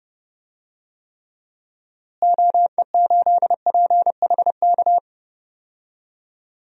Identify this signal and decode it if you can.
Morse code, heard as a keyed tone, about 30 wpm, 700 Hz. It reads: OE8PHK
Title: DXCC-100 Call Signs - 6 Characters - Repeated 3x - 30wpm